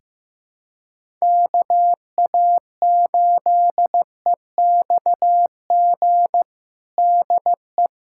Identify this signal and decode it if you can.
Morse code, heard as a keyed tone, about 15 wpm, 700 Hz. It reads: KA8EXG DE
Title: USA Callsign DE Callsign 15wpm